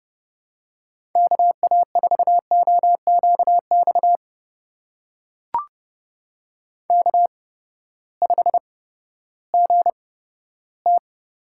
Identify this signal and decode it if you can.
Morse code, heard as a keyed tone, about 30 wpm, spaced as for 15 wpm, 700 Hz. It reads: KA4OQX K5GT
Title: USA Call Signs - 8x Character Spacing - 30wpm